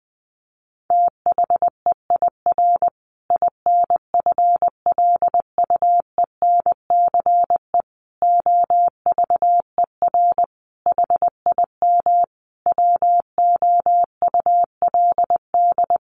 Morse code, heard as a keyed tone, about 19 wpm, 700 Hz. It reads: THEIR INFLUENCE OVER HIM WOULD
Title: Sentences from Top 1400 Words 20wpm